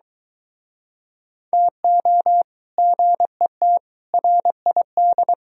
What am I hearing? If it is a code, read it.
Morse code, heard as a keyed tone, about 26 wpm, 700 Hz. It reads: TO GET RID